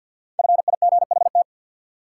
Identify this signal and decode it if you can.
Morse code, heard as a keyed tone, about 53 wpm, 700 Hz. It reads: FIGHT
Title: Five Letter Words - 50wpm